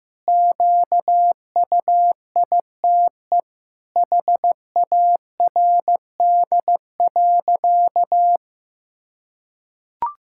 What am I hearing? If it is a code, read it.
Morse code, heard as a keyed tone, about 15 wpm, 700 Hz. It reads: QUITE HARD.
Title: Pangram Challenge v2 - 15wpm